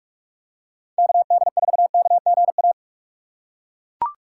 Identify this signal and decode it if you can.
Morse code, heard as a keyed tone, about 41 wpm, 700 Hz. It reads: KD4KGA